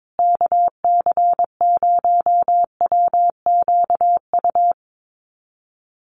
Morse code, heard as a keyed tone, about 22 wpm, 700 Hz. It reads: KC0WQU
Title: USA Call Signs Repeated 2x 22wpm